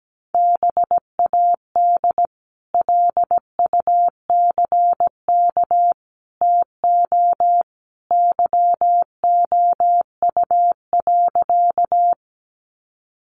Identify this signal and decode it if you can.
Morse code, heard as a keyed tone, about 17 wpm, 700 Hz. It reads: BAD LUCK TO YOU.
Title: Sentences from Top 1500 Words 17wpm